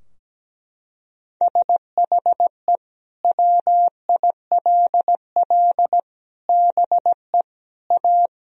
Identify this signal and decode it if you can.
Morse code, heard as a keyed tone, about 17 wpm, 700 Hz. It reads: SHE WILL BE A
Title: Sentences from Top 1000 Words - Review Part 4 - 17wpm